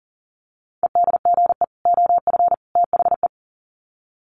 Morse code, heard as a keyed tone, about 38 wpm, 700 Hz. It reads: EDGE OF THE
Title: Sets of 3 Words 40wpm